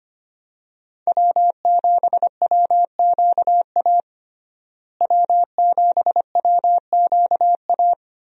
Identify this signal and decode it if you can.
Morse code, heard as a keyed tone, about 25 wpm, 700 Hz. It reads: W7WQA W7WQA
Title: USA Call Signs - 5 Characters - Repeated 3x - 25wpm